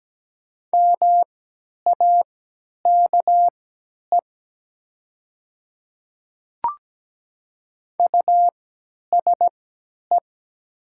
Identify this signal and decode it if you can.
Morse code, heard as a keyed tone, about 17 wpm, spaced as for 10 wpm, 690 Hz. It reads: MAKE USE
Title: 3x Character Spacing - Top 200 Words - 17wpm